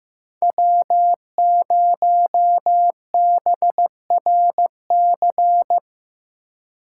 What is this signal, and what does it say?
Morse code, heard as a keyed tone, about 18 wpm, 700 Hz. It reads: W0BRC
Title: Speed Racing - USA 1x3 Call Signs 15wpm